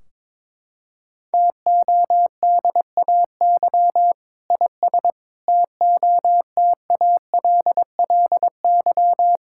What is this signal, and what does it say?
Morse code, heard as a keyed tone, about 22 wpm, 700 Hz. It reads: TODAY IS TOTALLY
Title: Sentences from Top 1600 Words 22wpm